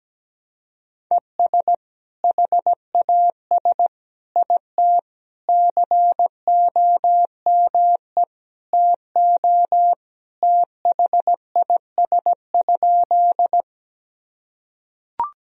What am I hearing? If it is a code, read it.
Morse code, heard as a keyed tone, about 17 wpm, 700 Hz. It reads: ES HAS IT COME TO THIS?
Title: Sentences from Top 100 Words and Abbreviations 17wpm